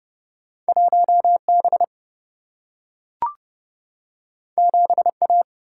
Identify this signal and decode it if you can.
Morse code, heard as a keyed tone, about 30 wpm, 700 Hz. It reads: 1B 7A
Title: Single-Letter Single-Number 30wpm